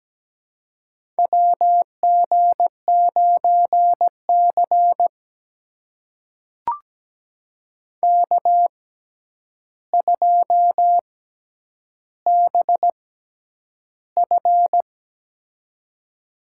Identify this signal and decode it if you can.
Morse code, heard as a keyed tone, about 17 wpm, spaced as for 9 wpm, 700 Hz. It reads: WG9C K2BF
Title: USA Call Signs - 4 Characters - 6x Character Spacing - 17wpm